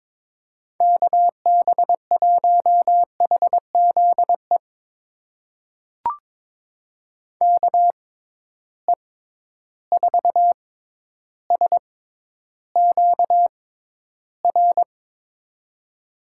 Morse code, heard as a keyed tone, about 22 wpm, spaced as for 11 wpm, 700 Hz. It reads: KB1HZE KE4SQR
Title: USA Call Signs - 6 Characters - 6x Character Spacing - 22wpm